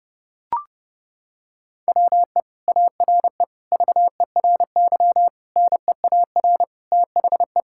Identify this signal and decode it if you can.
Morse code, heard as a keyed tone, about 30 wpm, 700 Hz. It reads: WE ARE VERY NEAR THE
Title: Sentences from Top 1000 Words - Review Part 6 - 30wpm